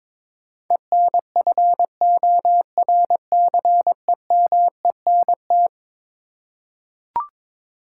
Morse code, heard as a keyed tone, about 22 wpm, 700 Hz. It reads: ENFORCEMENT
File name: practice / 22wpm / Eleven Letter Words - 22wpm